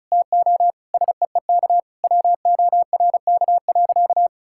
Morse code, heard as a keyed tone, about 36 wpm, 700 Hz. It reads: TO SEEK WORK.
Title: Sentences from Top 1600 Words 35wpm